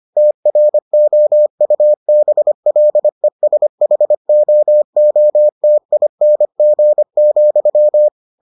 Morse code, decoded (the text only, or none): TROUBLESHOOTING,